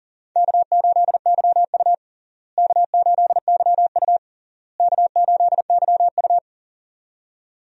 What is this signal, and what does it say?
Morse code, heard as a keyed tone, about 40 wpm, 700 Hz. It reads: K8YU K8YU K8YU
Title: USA Call Signs - 4 Characters - Repeated 3x - 40wpm